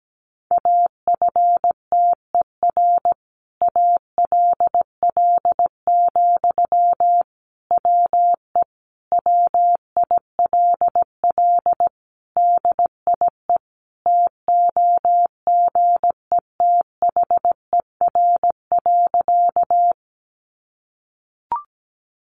Morse code, heard as a keyed tone, about 17 wpm, 700 Hz. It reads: AFTER ALL, WE WILL DIE TOGETHER.
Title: Sentences from Top 1300 Words 17wpm